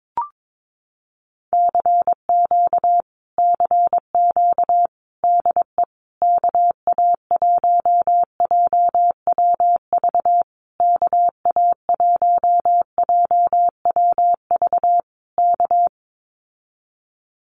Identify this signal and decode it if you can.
Morse code, heard as a keyed tone, about 22 wpm, 700 Hz. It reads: CQ CQ DE KA1JWV KA1JWV K
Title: Common QSO Phrases Encore 22wpm